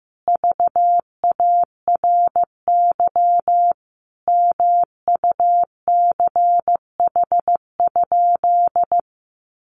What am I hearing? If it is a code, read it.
Morse code, heard as a keyed tone, about 15 wpm, 700 Hz. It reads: VARY MUCH?
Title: Sentences from Top 1200 Words 15wpm